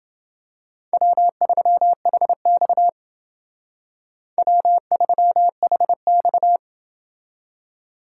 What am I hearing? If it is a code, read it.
Morse code, heard as a keyed tone, about 29 wpm, 700 Hz. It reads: W3HX W3HX